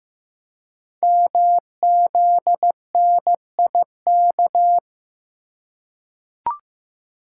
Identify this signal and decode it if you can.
Morse code, heard as a keyed tone, about 15 wpm, 700 Hz. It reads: MZNIK
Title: Code Groups - 5 Letters - 15wpm